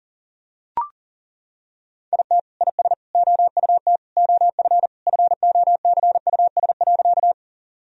Morse code, heard as a keyed tone, about 37 wpm, 700 Hz. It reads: IT IS OUT OF FOCUS.
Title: Sentences from Top 1000 Words - Review Part 4 - 40wpm